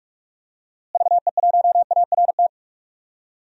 Morse code, heard as a keyed tone, about 42 wpm, 700 Hz. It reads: VE1ART
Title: DXCC-100 Call Signs - 6 Characters - Repeated 3x - 45wpm